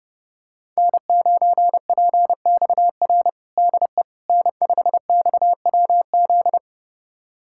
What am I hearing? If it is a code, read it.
Morse code, heard as a keyed tone, about 32 wpm, 700 Hz. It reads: N9PXR DE N5XWZ
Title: USA Callsign DE Callsign - 5 Characters - 30wpm